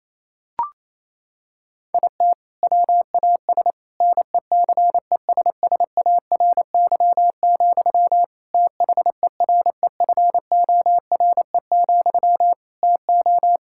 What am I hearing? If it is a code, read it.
Morse code, heard as a keyed tone, about 28 wpm, 700 Hz. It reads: IT WAS NECESSARY, THEREFORE, TO
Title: Sentences from Top 1600 Words 28wpm